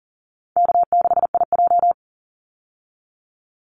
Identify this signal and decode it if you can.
Morse code, heard as a keyed tone, about 40 wpm, 700 Hz. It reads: K6IJ
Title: Easy Call Signs - 40wpm